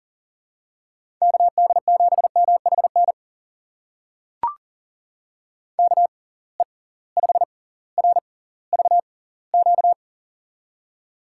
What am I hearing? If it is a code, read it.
Morse code, heard as a keyed tone, about 40 wpm, spaced as for 20 wpm, 700 Hz. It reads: KD7MHN KE5RVQ
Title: USA Call Signs - 6 Characters - 6x Character Spacing - 40wpm